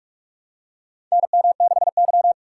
Morse code, heard as a keyed tone, about 46 wpm, 700 Hz. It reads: NM6Y
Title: USA Call Signs - 4 Characters - 6x Character Spacing - 45wpm